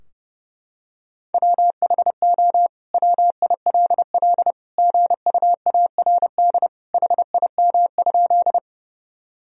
Morse code, heard as a keyed tone, about 30 wpm, 700 Hz. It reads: WHO WILL GUARD HIM?
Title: Sentences from Top 1000 Words - Review Part 4 - 30wpm